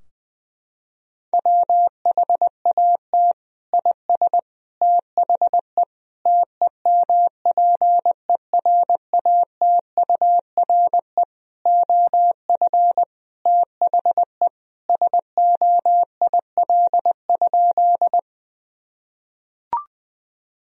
Morse code, heard as a keyed tone, about 20 wpm, 700 Hz. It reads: WHAT IS THE TEMPERATURE OF THE SOIL?
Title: Sentences from Top 1000 Words - Review Part 4 - 20wpm